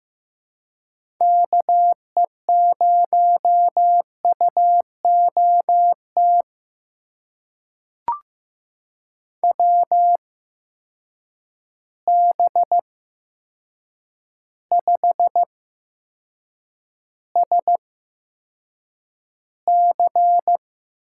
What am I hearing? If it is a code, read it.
Morse code, heard as a keyed tone, about 15 wpm, spaced as for 7 wpm, 700 Hz. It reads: KE0UOT WB5SC
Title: USA Call Signs - 6 Characters - 8x Character Spacing - 15wpm